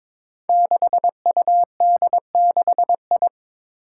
Morse code, heard as a keyed tone, about 22 wpm, 700 Hz. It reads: 6UD6I
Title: Five Challenging Letter-Number Combinations 22wpm